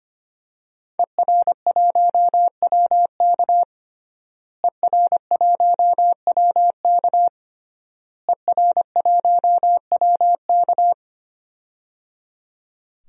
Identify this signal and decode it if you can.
Morse code, heard as a keyed tone, about 25 wpm, 700 Hz. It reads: ER1WK ER1WK ER1WK
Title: DXCC-100 Call Signs - 5 Characters - Repeated 3x - 25wpm